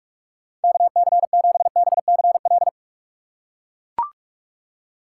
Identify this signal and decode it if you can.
Morse code, heard as a keyed tone, about 42 wpm, 700 Hz. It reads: KC7BCL
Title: USA Call Signs - 6x Character Spacing - 45wpm